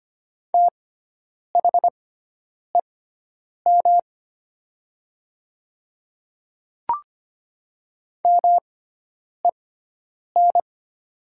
Morse code, heard as a keyed tone, about 25 wpm, spaced as for 9 wpm, 700 Hz. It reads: THEM MEN
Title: 6x Character Spacing - Top 200 Words - 25wpm